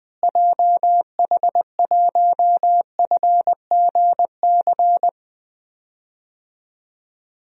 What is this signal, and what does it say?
Morse code, heard as a keyed tone, about 20 wpm, 700 Hz. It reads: JH1FGC
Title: DXCC-100 Call Signs - 6 Characters - Repeated 3x - 20wpm